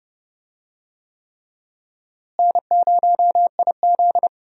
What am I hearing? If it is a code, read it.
Morse code, heard as a keyed tone, about 31 wpm, 700 Hz. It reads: N0IZ
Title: USA Call Signs - 4 Characters - Repeated 3x - 30wpm